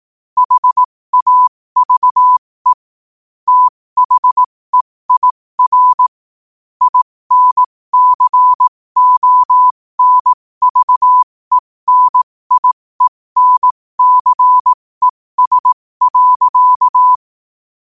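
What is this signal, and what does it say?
Morse code, heard as a keyed tone, about 18 wpm, spaced as for 15 wpm, 1000 Hz. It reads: HAVE THEIR INCONVENIENCES.
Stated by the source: Morse code quote of the day sent at 15 words per minute.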